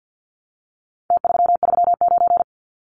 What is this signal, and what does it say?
Morse code, heard as a keyed tone, about 49 wpm, 700 Hz. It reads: T339
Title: Numbers - Contest Style - 4-Digit 50wpm